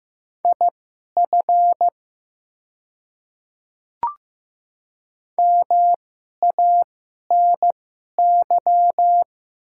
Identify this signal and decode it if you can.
Morse code, heard as a keyed tone, about 15 wpm, spaced as for 11 wpm, 700 Hz. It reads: IF MANY
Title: 2x Character Spacing - Top 500 Words - 15wpm